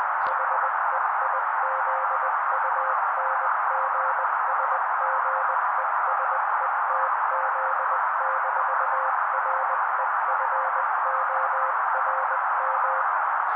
Morse code, heard as a keyed tone, about 20 wpm, 530 Hz. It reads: HEIZUNGSGESETZ-REFORM
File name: daily / 18wpm noise